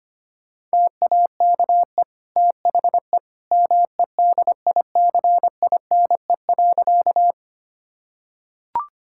Morse code, heard as a keyed tone, about 25 wpm, 700 Hz. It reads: TAKE THE MEDICINE.
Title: Sentences from Top 1000 Words - Review Part 6 - 25wpm